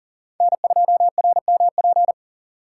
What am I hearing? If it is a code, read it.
Morse code, heard as a keyed tone, about 40 wpm, 700 Hz. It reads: N2RMP
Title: USA Call Signs - 5 Characters - 6x Character Spacing - 40wpm